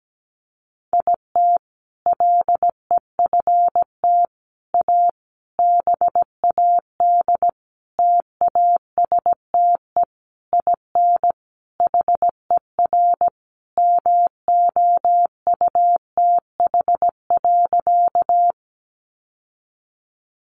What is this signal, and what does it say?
Morse code, heard as a keyed tone, about 17 wpm, 700 Hz. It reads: IT LEFT A BAD TASTE IN HER MOUTH.
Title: Sentences from Top 1500 Words 17wpm